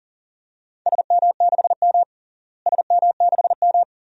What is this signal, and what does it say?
Morse code, heard as a keyed tone, about 40 wpm, 700 Hz. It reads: SM6M SM6M
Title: DXCC-100 Call Signs - 4 Characters - Repeated 2x - 40wpm